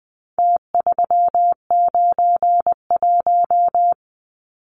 Morse code, heard as a keyed tone, about 20 wpm, 700 Hz. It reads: T391
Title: Numbers - Contest Style - 4-Digit 20wpm